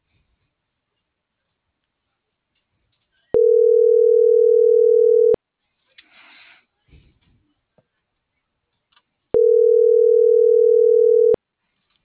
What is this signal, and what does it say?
Ringback tone (North America), 440 + 480 Hz.